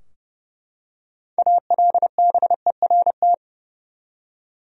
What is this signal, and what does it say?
Morse code, heard as a keyed tone, about 30 wpm, 700 Hz. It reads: ALBERT
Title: USA Common Names - 30wpm